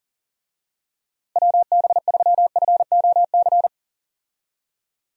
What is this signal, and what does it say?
Morse code, heard as a keyed tone, about 40 wpm, 700 Hz. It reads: WB3FOC
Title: USA Call Signs - 6 Characters - Repeated 2x - 40wpm